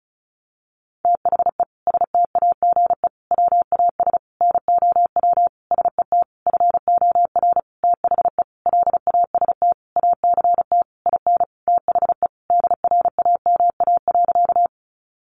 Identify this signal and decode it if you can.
Morse code, heard as a keyed tone, about 34 wpm, 700 Hz. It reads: THE STAGE WAS NOW SET FOR THE LAST ACT IN THE DRAMA.
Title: Sentences from Top 1500 Words 35wpm